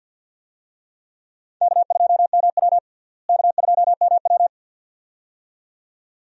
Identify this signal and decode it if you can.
Morse code, heard as a keyed tone, about 49 wpm, 700 Hz. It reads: K2MW K2MW
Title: USA Call Signs - 4 Characters - Repeated 2x - 50wpm